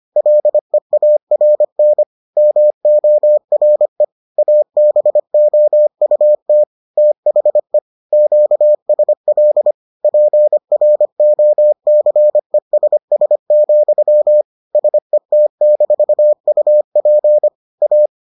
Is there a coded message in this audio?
Yes: Morse code from 0.2 to 18.1 s, heard as a keyed tone, about 25 wpm, 600 Hz. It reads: LEARN MORE ABOUT THE QSL PROCESS, SET-UP A